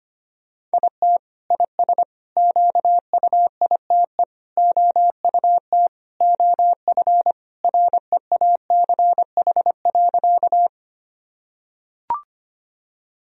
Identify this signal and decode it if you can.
Morse code, heard as a keyed tone, about 25 wpm, 700 Hz. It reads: IT IS QUITE OUT OF REACH.
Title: Sentences from Top 1000 Words - Review Part 2 - 25wpm